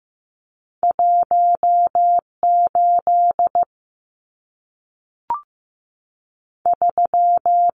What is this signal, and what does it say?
Morse code, heard as a keyed tone, about 15 wpm, spaced as for 3 wpm, 700 Hz. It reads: 1 83
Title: Numbers - 2-Digit 15wpm